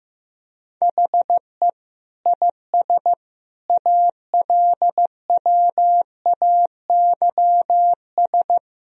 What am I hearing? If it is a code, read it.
Morse code, heard as a keyed tone, about 15 wpm, 700 Hz. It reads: HE IS ALWAYS
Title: Sentences from Top 1000 Words - Review Part 3 - 15wpm